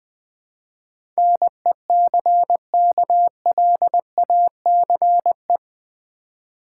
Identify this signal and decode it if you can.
Morse code, heard as a keyed tone, about 20 wpm, 700 Hz. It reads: NECKLACE
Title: Easy Words with Hints - 20wpm